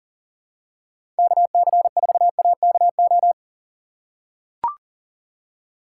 Morse code, heard as a keyed tone, about 40 wpm, 700 Hz. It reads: KC4AKO
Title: USA Call Signs - 6 Characters - 6x Character Spacing - 40wpm